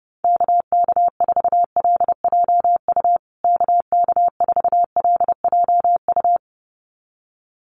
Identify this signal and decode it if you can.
Morse code, heard as a keyed tone, about 30 wpm, 700 Hz. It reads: KK4LJU KK4LJU
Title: USA Call Signs Repeated 2x 30wpm